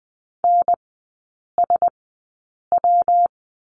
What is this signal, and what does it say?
Morse code, heard as a keyed tone, about 20 wpm, spaced as for 8 wpm, 700 Hz. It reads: NSW
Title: 114 Lesson 19e - Letter w - Three Characters 20wpm